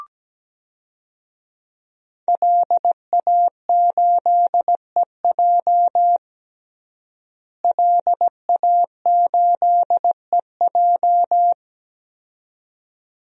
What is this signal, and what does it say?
Morse code, heard as a keyed tone, about 17 wpm, 700 Hz. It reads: LA8EJ LA8EJ
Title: DXCC-100 Call Signs - 5 Characters - Repeated 2x - 17wpm